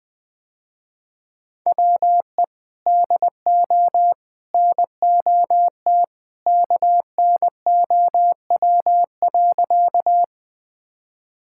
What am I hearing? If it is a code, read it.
Morse code, heard as a keyed tone, about 20 wpm, 700 Hz. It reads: WE DO NOT KNOW.
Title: Sentences with 3-Word Phrases - Spoken Beginning - 20wpm